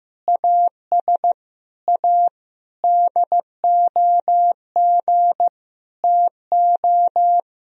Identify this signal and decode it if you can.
Morse code, heard as a keyed tone, about 15 wpm, 700 Hz. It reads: AS A DOG TO
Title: Sentences from Top 1000 Words - Review Part 2 - 15wpm